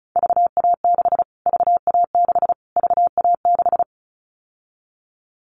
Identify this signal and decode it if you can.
Morse code, heard as a keyed tone, about 34 wpm, 700 Hz. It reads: VA6 VA6 VA6
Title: Callsign Prefixes for Top 25 DXCC - Repeated 3x - 35wpm